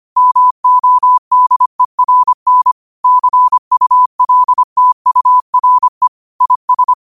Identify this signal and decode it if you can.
Morse code, heard as a keyed tone, about 25 wpm, 1000 Hz. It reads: MODERN CULTURE IS
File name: qotdfiles / 25wpm